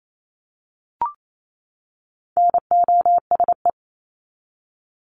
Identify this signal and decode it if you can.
Morse code, heard as a keyed tone, about 28 wpm, 700 Hz. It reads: NOSE
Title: Top 1501-1750 Words 28wpm